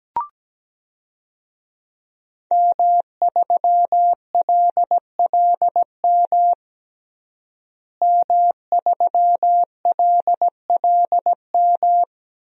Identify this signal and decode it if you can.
Morse code, heard as a keyed tone, about 17 wpm, 700 Hz. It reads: M3LLM M3LLM
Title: DXCC-100 Call Signs - 5 Characters - Repeated 2x - 17wpm